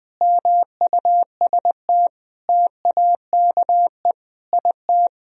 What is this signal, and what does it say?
Morse code, heard as a keyed tone, about 20 wpm, 700 Hz. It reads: MUST TAKE IT
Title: Sentences from Top 500 Words and Abbreviations 20wpm